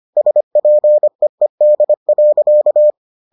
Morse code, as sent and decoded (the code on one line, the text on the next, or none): ... .--. . -.. .-.-.-
SPEED.